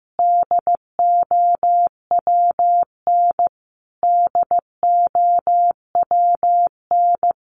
Morse code, heard as a keyed tone, about 15 wpm, 700 Hz. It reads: DOWN DOWN
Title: Top 200 Words Repeated 15wpm